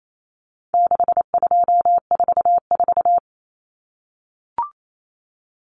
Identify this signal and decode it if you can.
Morse code, heard as a keyed tone, about 28 wpm, 700 Hz. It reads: 6244
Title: Numbers - 4-Digit 28wpm